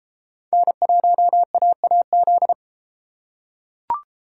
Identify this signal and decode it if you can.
Morse code, heard as a keyed tone, about 32 wpm, 700 Hz. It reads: N1AAZ